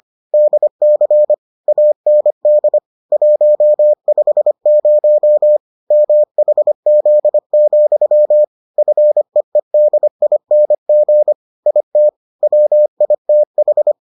Morse code, heard as a keyed tone, about 25 wpm, 600 Hz. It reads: DC AND 150 MHZ, FEEDING IT WITH